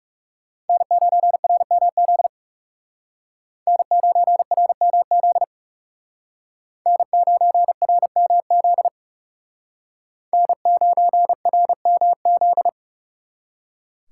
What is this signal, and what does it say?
Morse code, heard as a keyed tone, about 38 wpm, 700 Hz. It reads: N9RMZ N9RMZ N9RMZ N9RMZ